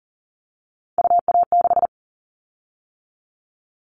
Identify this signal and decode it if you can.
Morse code, heard as a keyed tone, about 38 wpm, 700 Hz. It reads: UA6
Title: Callsign Prefixes for Top 100 DXCC - 40wpm